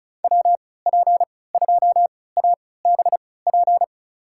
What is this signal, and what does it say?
Morse code, heard as a keyed tone, about 36 wpm, spaced as for 20 wpm, 700 Hz. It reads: WP2ABP
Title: USA Call Signs - 3x Character Spacing - 35wpm